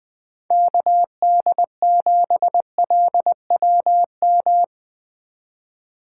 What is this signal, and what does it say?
Morse code, heard as a keyed tone, about 20 wpm, 700 Hz. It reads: KD7LWM
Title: Speed Racing - USA 2x3 Call Signs 17wpm